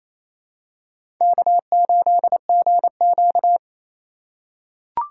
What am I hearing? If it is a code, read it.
Morse code, heard as a keyed tone, about 28 wpm, 700 Hz. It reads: K8GQ
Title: USA Call Signs - 4 Characters - 4x Character Spacing - 28wpm